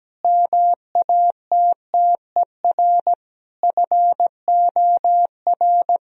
Morse code, heard as a keyed tone, about 17 wpm, 700 Hz. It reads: MATTER FOR
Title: Sentences from Top 1000 Words - Review Part 4 - 17wpm